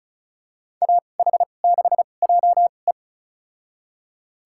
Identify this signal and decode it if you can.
Morse code, heard as a keyed tone, about 36 wpm, spaced as for 25 wpm, 700 Hz. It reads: AH6JE
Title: USA Call Signs - 5 Characters - 2x Character Spacing - 35wpm